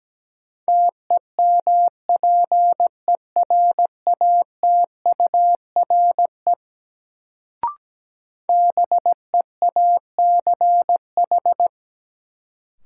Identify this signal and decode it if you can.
Morse code, heard as a keyed tone, about 17 wpm, 700 Hz. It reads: TEMPERATURE BEACH
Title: Sentences from Top 1500 Words - New Word Drill Alt - 17wpm